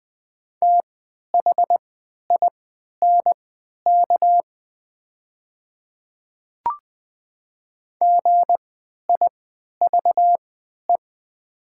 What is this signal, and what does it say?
Morse code, heard as a keyed tone, about 20 wpm, spaced as for 11 wpm, 700 Hz. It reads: THINK GIVE
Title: 3x Character Spacing - Top 500 Words - 20wpm